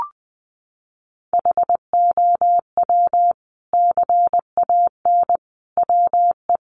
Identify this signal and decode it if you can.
Morse code, heard as a keyed tone, about 20 wpm, 700 Hz. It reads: HOW CAN WE
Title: Sentences from Top 1400 Words 20wpm